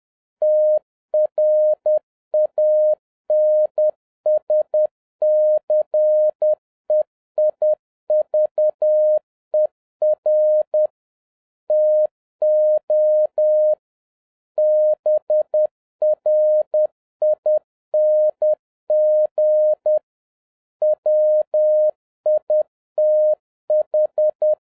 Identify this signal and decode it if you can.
Morse code, heard as a keyed tone, about 10 wpm, 610 Hz. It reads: TRANSCEIVER TO BRING WITH